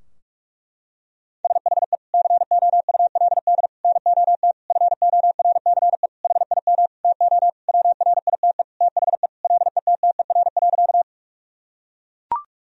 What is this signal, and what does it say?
Morse code, heard as a keyed tone, about 41 wpm, 700 Hz. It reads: SHE COULD NOT FORCE HIM TO WRITE THE LETTER.
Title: Sentences from Top 1000 Words - Review Part 3 - 45wpm